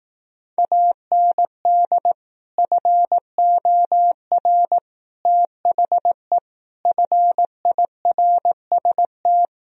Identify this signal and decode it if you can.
Morse code, heard as a keyed tone, about 18 wpm, 700 Hz. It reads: AND FOR THE FIRST
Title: Speed-Racing - Sets of 5 Words 15wpm